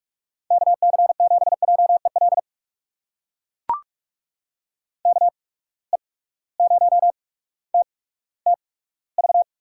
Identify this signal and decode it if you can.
Morse code, heard as a keyed tone, about 44 wpm, spaced as for 19 wpm, 700 Hz. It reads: KC7JEL KE0TTV
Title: USA Call Signs - 6 Characters - 8x Character Spacing - 45wpm